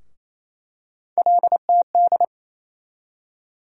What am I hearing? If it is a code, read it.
Morse code, heard as a keyed tone, about 28 wpm, 700 Hz. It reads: LTD
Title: USA Call Sign Postfixes - 3 Letters 28wpm